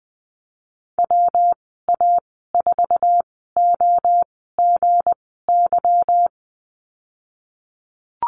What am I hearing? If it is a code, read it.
Morse code, heard as a keyed tone, about 20 wpm, spaced as for 15 wpm, 700 Hz. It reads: WA4OGY
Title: 173 Lesson 28g - Number 9 - Call Signs 20wpm